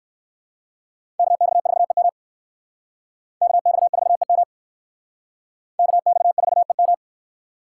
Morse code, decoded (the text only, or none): KX4EM KX4EM KX4EM